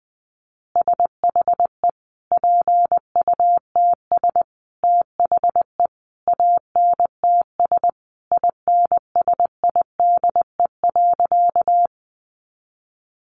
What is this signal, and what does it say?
Morse code, heard as a keyed tone, about 20 wpm, 700 Hz. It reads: SHE PUTS THE ANTS INSIDE.
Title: Easy Phrases 20wpm